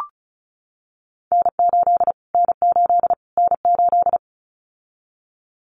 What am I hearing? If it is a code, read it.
Morse code, heard as a keyed tone, about 34 wpm, 700 Hz. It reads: N8 N8 N8
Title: Callsign Prefixes for Top 25 DXCC - Repeated 3x - 35wpm